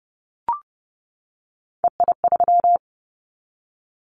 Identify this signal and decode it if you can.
Morse code, heard as a keyed tone, about 30 wpm, 700 Hz. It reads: EI3
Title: Callsign Prefixes for Top 50 DXCC - 30wpm